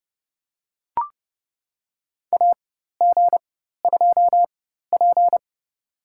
Morse code, heard as a keyed tone, about 30 wpm, spaced as for 14 wpm, 700 Hz. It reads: AG2P
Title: USA Call Signs - 4 Characters - 4x Character Spacing - 30wpm